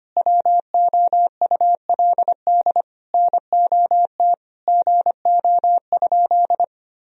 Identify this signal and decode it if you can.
Morse code, heard as a keyed tone, about 25 wpm, 700 Hz. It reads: WOULD NOT GO?
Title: Sentences from Top 1000 Words - Review Part 4 - 25wpm